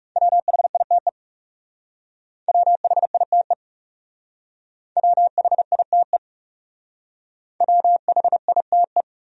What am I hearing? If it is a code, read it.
Morse code, heard as a keyed tone, about 36 wpm, 700 Hz. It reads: WHITE WHITE WHITE WHITE
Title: Speed-Racing Top 500 Words 30wpm